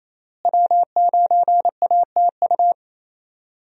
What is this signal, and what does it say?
Morse code, heard as a keyed tone, about 27 wpm, 700 Hz. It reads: W9ATU
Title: USA Call Signs - 5 Characters - Repeated 3x - 28wpm